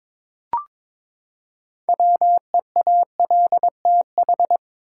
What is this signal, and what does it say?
Morse code, heard as a keyed tone, about 22 wpm, 700 Hz. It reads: WEALTH
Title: Six Letter Words - 22wpm